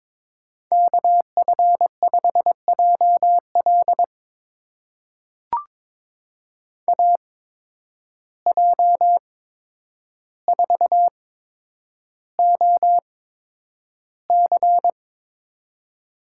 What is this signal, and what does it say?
Morse code, heard as a keyed tone, about 23 wpm, spaced as for 10 wpm, 700 Hz. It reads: KF5JL AJ4OC
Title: USA Call Signs - 5 Characters - 8x Character Spacing - 22wpm